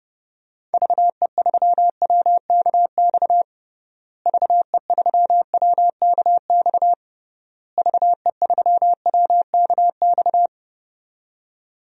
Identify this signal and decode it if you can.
Morse code, heard as a keyed tone, about 30 wpm, 700 Hz. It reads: VE3WKX VE3WKX VE3WKX
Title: DXCC-100 Call Signs - 6 Characters - Repeated 3x - 30wpm